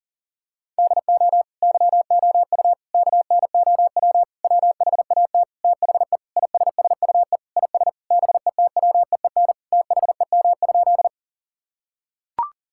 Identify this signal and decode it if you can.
Morse code, heard as a keyed tone, about 40 wpm, 700 Hz. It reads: DO YOU KNOW WHAT THE ISSUE IS BETWEEN THEM?
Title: Sentences from Top 1000 Words - Review Part 6 - 40wpm